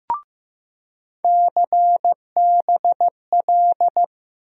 Morse code, heard as a keyed tone, about 15 wpm, 700 Hz. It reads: CBL
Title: USA Call Sign Postfixes - 3 Letters 15wpm